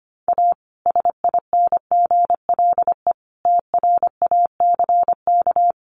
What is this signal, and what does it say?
Morse code, heard as a keyed tone, about 25 wpm, 700 Hz. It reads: A SINGLE TRACK
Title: Sentences from Top 1100 Words 25wpm